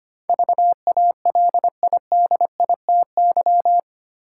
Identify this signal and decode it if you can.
Morse code, heard as a keyed tone, about 25 wpm, 700 Hz. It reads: VALIDITY
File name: 8 Letter Words - First 3 Letters Given - 25wpm